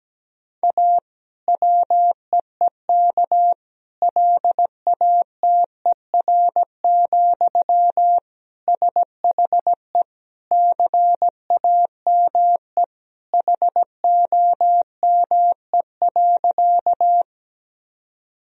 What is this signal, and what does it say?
Morse code, heard as a keyed tone, about 17 wpm, 700 Hz. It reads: A WEEK LATER, SHE CAME HOME.
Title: Sentences from Top 1000 Words - Review Part 4 - 17wpm